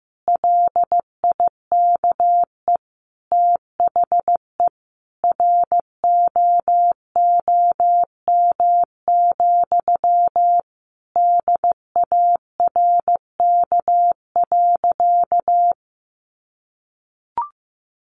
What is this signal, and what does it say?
Morse code, heard as a keyed tone, about 15 wpm, 700 Hz. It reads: LIKE THE ROOM, DARK.
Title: Sentences from Top 1400 Words 15wpm